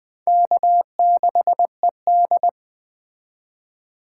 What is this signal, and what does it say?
Morse code, heard as a keyed tone, about 21 wpm, 700 Hz. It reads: K6ED
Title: Speed Racing - USA 1x2 Call Signs 17wpm